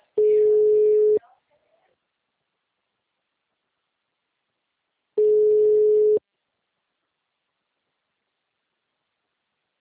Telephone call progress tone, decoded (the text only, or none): ringback tone